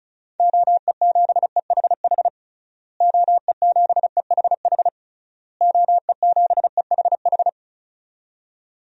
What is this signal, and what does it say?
Morse code, heard as a keyed tone, about 36 wpm, 700 Hz. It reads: OE7EHH OE7EHH OE7EHH
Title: DXCC-100 Call Signs - 6 Characters - Repeated 3x - 35wpm